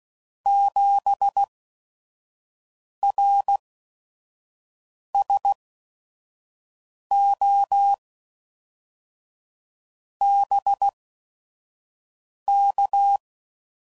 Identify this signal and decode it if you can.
Morse code, heard as a keyed tone, about 16 wpm, spaced as for 5 wpm, 800 Hz. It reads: 7RSO BK
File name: CW Code Groups @5WPM / QSOs / QSO WA7QOX 1